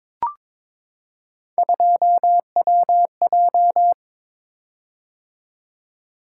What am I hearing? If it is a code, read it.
Morse code, heard as a keyed tone, about 22 wpm, 700 Hz. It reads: 2WJ
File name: practice / 22wpm / Three Letter-Number Combinations - 22wpm